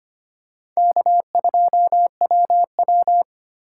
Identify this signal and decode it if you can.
Morse code, heard as a keyed tone, about 25 wpm, 700 Hz. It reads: K2WW
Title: USA Call Signs - 4 Characters - 4x Character Spacing - 25wpm